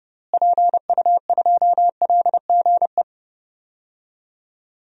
Morse code, heard as a keyed tone, about 30 wpm, 700 Hz. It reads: PU2LGE
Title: DXCC-100 Call Signs - 6 Characters - Repeated 2x - 30wpm